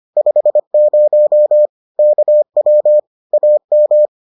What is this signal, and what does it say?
Morse code, heard as a keyed tone, about 25 wpm, 600 Hz. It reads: 50 KW AM